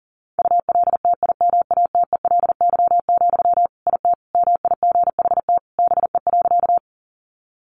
Morse code, heard as a keyed tone, about 37 wpm, 700 Hz. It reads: ULTIMATELY, IT MIGHT BE.
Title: Sentences from Top 1200 Words 40wpm